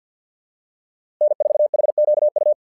Morse code, heard as a keyed tone, about 49 wpm, 600 Hz. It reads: N4HQU
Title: SOTA QSO Practice 50wpm